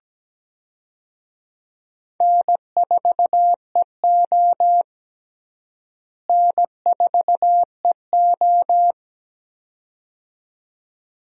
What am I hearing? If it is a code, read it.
Morse code, heard as a keyed tone, about 17 wpm, 700 Hz. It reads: N4EO N4EO
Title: USA Call Signs - 4 Characters - Repeated 2x - 17wpm